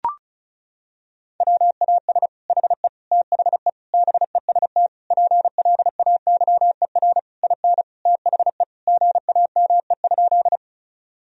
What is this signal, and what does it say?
Morse code, heard as a keyed tone, about 36 wpm, 700 Hz. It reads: WAS HE THE BEST PLAYER IN THE GAME?
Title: Sentences from Top 1000 Words - Review Part 4 - 35wpm